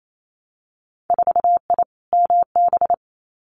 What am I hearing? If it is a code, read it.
Morse code, heard as a keyed tone, about 28 wpm, 700 Hz. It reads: 4I MB
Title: Winter Field Day Exchanges Encore 28wpm